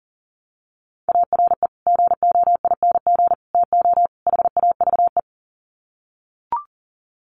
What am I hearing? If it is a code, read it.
Morse code, heard as a keyed tone, about 40 wpm, 700 Hz. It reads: ARE GOING TO HAVE
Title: Sets of 4 Words 40wpm